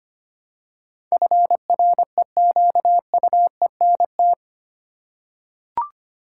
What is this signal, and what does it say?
Morse code, heard as a keyed tone, about 25 wpm, 690 Hz. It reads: FREQUENT
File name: 8 Letter Words - First 3 Letters Given - 25wpm